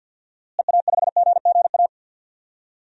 Morse code, heard as a keyed tone, about 52 wpm, 700 Hz. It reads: EA5GGA
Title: DXCC-100 Call Signs - 6 Characters - Repeated 3x - 50wpm